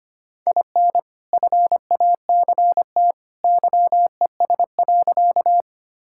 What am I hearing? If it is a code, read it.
Morse code, heard as a keyed tone, about 25 wpm, 690 Hz. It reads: IN FACT YES.
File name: practice / 25wpm / Sentences with 2-Word Phrases - 25wpm